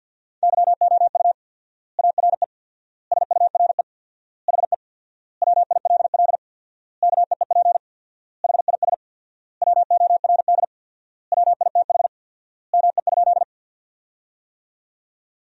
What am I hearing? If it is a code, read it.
Morse code, heard as a keyed tone, about 50 wpm, 700 Hz. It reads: YOU ARE SURE HE WILL KEEP HIS WORD WITH ME?
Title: 4x Word-Spacing - Sentences from Top 300 Words - 50wpm